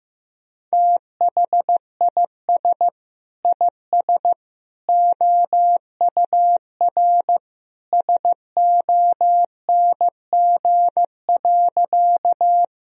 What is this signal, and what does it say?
Morse code, heard as a keyed tone, about 15 wpm, 700 Hz. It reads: THIS IS OUR SONG.
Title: Sentences from Top 1000 Words - Review Part 2 - 15wpm